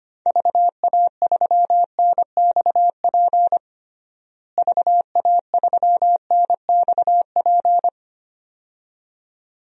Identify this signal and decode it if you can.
Morse code, heard as a keyed tone, about 25 wpm, 700 Hz. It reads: VA3NXP VA3NXP
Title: DXCC-100 Call Signs - 6 Characters - Repeated 2x - 25wpm